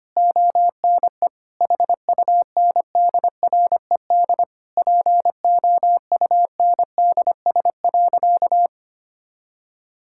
Morse code, heard as a keyed tone, about 25 wpm, 700 Hz. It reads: ONE HUNDRED POUNDS.
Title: Sentences from Top 1000 Words - Review Part 6 - 25wpm